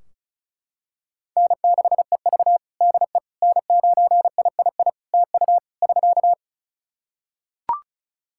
Morse code, heard as a keyed tone, about 36 wpm, 700 Hz. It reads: N6EV DE N9III TU <SK>
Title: Practice QSO - Exchange 4 - Ending - 35wpm